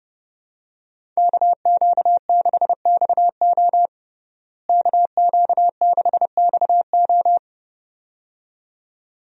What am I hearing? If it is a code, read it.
Morse code, heard as a keyed tone, about 32 wpm, 700 Hz. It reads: KQ6XO KQ6XO
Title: USA Call Signs - 5 Characters - Repeated 2x - 30wpm